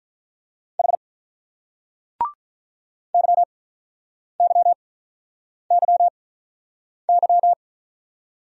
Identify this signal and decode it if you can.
Morse code, heard as a keyed tone, about 43 wpm, spaced as for 8 wpm, 700 Hz. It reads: H YYYY
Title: Speed-Racing Single Letters 35wpm